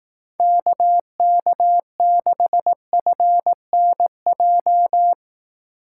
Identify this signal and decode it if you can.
Morse code, heard as a keyed tone, about 18 wpm, 700 Hz. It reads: KK6FNJ
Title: Speed Racing - USA 2x3 Call Signs 15wpm